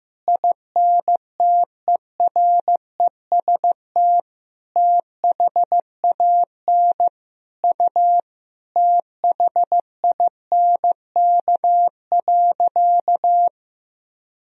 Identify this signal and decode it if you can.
Morse code, heard as a keyed tone, about 15 wpm, 700 Hz. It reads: INTEREST THAN U THINK.
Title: Sentences from Top 400 Words and Abbreviations 15wpm